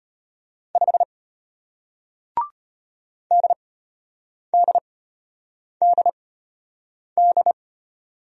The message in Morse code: 5 DDDD

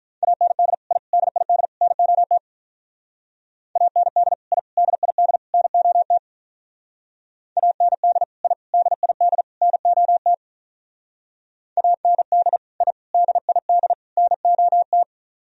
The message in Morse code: AND I DID NOT AND I DID NOT AND I DID NOT AND I DID NOT

